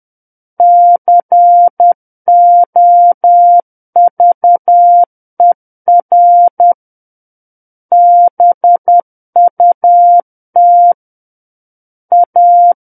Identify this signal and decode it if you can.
Morse code, heard as a keyed tone, about 10 wpm, 700 Hz. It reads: COVER BUT A
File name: Book One / 01-Chapter 1 - 10 WPM